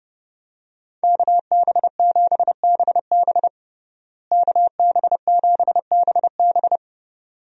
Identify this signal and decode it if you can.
Morse code, heard as a keyed tone, about 30 wpm, 700 Hz. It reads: KB7BB KB7BB
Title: USA Call Signs - 5 Characters - Repeated 2x - 30wpm